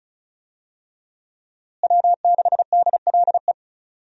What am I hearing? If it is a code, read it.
Morse code, heard as a keyed tone, about 36 wpm, 700 Hz. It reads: W6DLE
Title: USA Call Signs - 5 Characters - Repeated 3x - 35wpm